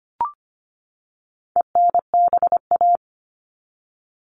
Morse code, heard as a keyed tone, about 25 wpm, 700 Hz. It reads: ENBA
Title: All Cut Numbers - 4-Digit 25wpm